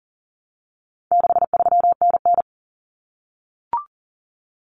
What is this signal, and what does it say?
Morse code, heard as a keyed tone, about 40 wpm, 700 Hz. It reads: 63NN
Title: Common Cut Numbers - 4-Digit 40wpm